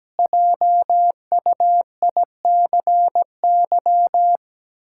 Morse code, heard as a keyed tone, about 17 wpm, 700 Hz. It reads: JUICY
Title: Pangram Challenge v2 - 17wpm